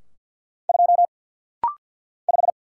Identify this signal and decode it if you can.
Morse code, heard as a keyed tone, about 50 wpm, 700 Hz. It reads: 2 5
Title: Single Letter-Number Alt - Rapid-Fire 50wpm